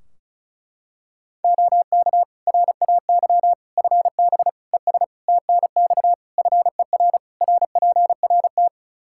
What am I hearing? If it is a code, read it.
Morse code, heard as a keyed tone, about 36 wpm, 700 Hz. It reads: OK RAY FB ES TNX FER RPRT
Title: Practice QSO - Exchange 2 - Info - 35wpm